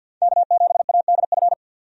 Morse code, heard as a keyed tone, about 52 wpm, 700 Hz. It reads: K7ADF
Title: USA Call Signs - 5 Characters - 50wpm